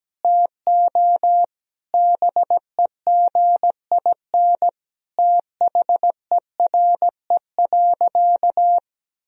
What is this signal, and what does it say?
Morse code, heard as a keyed tone, about 17 wpm, 700 Hz. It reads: TO BEGIN THERE.
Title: Sentences from Top 1000 Words - Review Part 1 - 17wpm